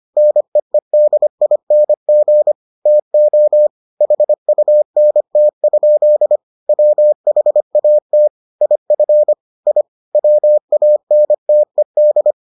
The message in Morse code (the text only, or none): NEEDING TO HUNT? WHAT IF I WANTED